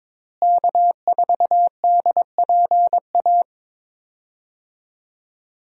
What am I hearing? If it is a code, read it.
Morse code, heard as a keyed tone, about 22 wpm, 700 Hz. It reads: K4DPA
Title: USA Call Signs - 5 Characters - Repeated 2x - 22wpm